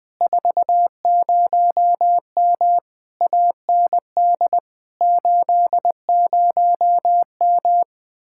Morse code, heard as a keyed tone, about 20 wpm, 700 Hz. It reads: 40M AND 80M
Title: Speed-Racing Common QSO Phrases Encore - 17wpm